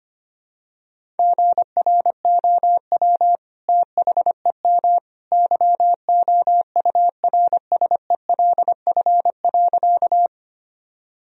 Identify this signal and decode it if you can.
Morse code, heard as a keyed tone, about 25 wpm, 700 Hz. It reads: GROW THEM YOURSELF.
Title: Sentences from Top 1000 Words - Review Part 1 - 25wpm